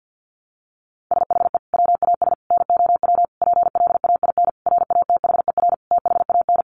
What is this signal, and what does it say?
Morse code, heard as a keyed tone, about 48 wpm, 700 Hz. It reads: SHE WAS NOW PLAIN RATHER THAN
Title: Sentences from Top 1000 Words 50wpm